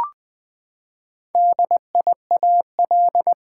Morse code, heard as a keyed tone, about 20 wpm, 700 Hz. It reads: DIAL
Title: Build A Word - 4 Letters At A Time - 8-Letter Words - 20wpm